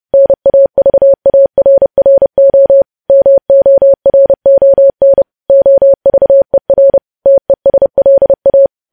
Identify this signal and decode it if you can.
Morse code, heard as a keyed tone, about 30 wpm, 550 Hz. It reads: NAVARRO MORON OVER TESLA
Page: news headlines in morse code at 30 wpm